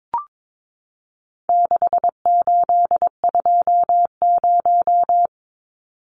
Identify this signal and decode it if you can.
Morse code, heard as a keyed tone, about 22 wpm, 700 Hz. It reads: 6820
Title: Numbers - 4-Digit 22wpm